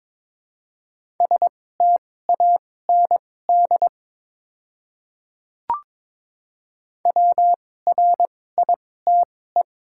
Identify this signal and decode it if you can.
Morse code, heard as a keyed tone, about 22 wpm, spaced as for 16 wpm, 700 Hz. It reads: STAND WRITE
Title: 2x Character Spacing - Top 300 Words - 22wpm